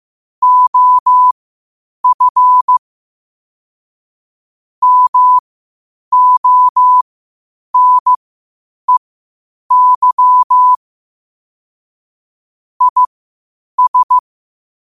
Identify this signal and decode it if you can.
Morse code, heard as a keyed tone, about 15 wpm, spaced as for 8 wpm, 1000 Hz. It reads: OF MONEY IS
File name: qotdfiles / 8wpm